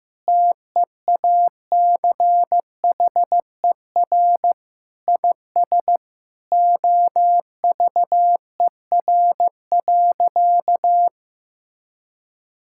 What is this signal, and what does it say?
Morse code, heard as a keyed tone, about 15 wpm, 700 Hz. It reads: TEACHER IS OVER.
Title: Sentences from Top 1600 Words 15wpm